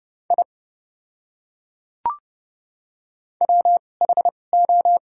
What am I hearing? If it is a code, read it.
Morse code, heard as a keyed tone, about 30 wpm, spaced as for 22 wpm, 700 Hz. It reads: I WHO
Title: 2x Character Spacing - Top 200 Words - 30wpm